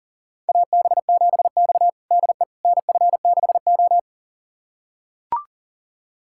Morse code, heard as a keyed tone, about 40 wpm, 700 Hz. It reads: AB7X DE NF6O
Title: USA Callsign DE Callsign - 4 Characters - 40wpm